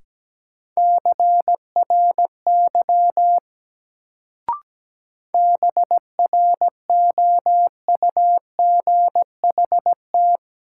Morse code, heard as a keyed tone, about 17 wpm, 700 Hz. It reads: CRY BROUGHT
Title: Sentences from Top 500 Words - New Word Drill - 17wpm